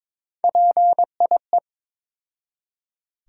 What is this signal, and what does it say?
Morse code, heard as a keyed tone, about 23 wpm, 700 Hz. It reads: PIE
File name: practice / 22wpm / Difficult Words with Hints - 22wpm